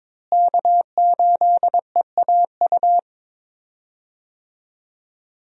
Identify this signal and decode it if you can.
Morse code, heard as a keyed tone, about 22 wpm, 700 Hz. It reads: K8EAU
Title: USA Call Signs - 5 Characters - Repeated 2x - 22wpm